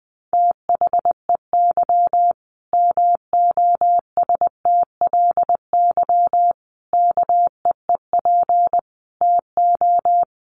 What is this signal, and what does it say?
Morse code, heard as a keyed tone, about 19 wpm, 700 Hz. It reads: THEY MOSTLY KEEP TO
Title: Sentences from Top 1300 Words 20wpm